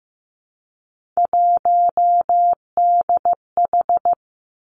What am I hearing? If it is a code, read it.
Morse code, heard as a keyed tone, about 15 wpm, 700 Hz. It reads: 1DH
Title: Three Challenging Letter-Number Combinations 15wpm